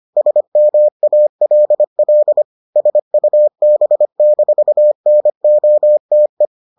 Morse code, heard as a keyed tone, about 25 wpm, 600 Hz. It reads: SMALL SUB-NOTE